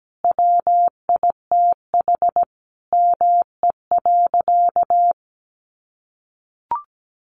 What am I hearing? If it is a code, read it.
Morse code, heard as a keyed tone, about 17 wpm, 700 Hz. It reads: WITH ME.
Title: Sentences from Top 700 Words 17wpm